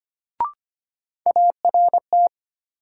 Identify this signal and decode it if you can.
Morse code, heard as a keyed tone, about 25 wpm, 700 Hz. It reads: ART
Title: Sentences from Top 600 Words - New Word Drill - 25wpm